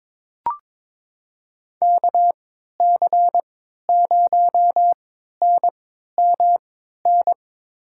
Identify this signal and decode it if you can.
Morse code, heard as a keyed tone, about 23 wpm, spaced as for 13 wpm, 700 Hz. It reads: KC0NMN
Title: USA Call Signs - 6 Characters - 3x Character Spacing - 22wpm